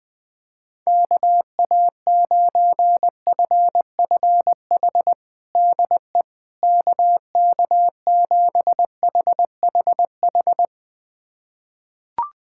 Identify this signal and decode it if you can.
Morse code, heard as a keyed tone, about 21 wpm, 700 Hz. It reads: KA9FFH DE KK7HHH
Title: USA Callsign DE Callsign - 6 Characters - 20wpm